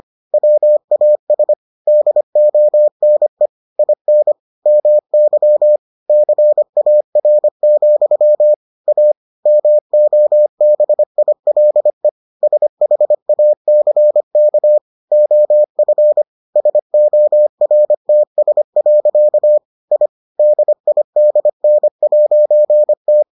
Morse code, heard as a keyed tone, about 25 wpm, 600 Hz. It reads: WAS DONE IN MY CAR, A MOBILE SHACK OF SORTS. I DIDN'T